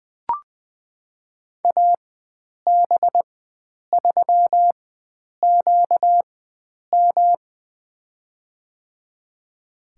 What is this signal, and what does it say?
Morse code, heard as a keyed tone, about 20 wpm, spaced as for 9 wpm, 700 Hz. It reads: AB3QM
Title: USA Call Signs - 5 Characters - 4x Character Spacing - 20wpm